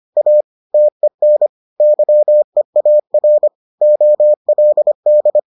Morse code, heard as a keyed tone, about 25 wpm, 600 Hz. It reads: A TEN YEAR OLD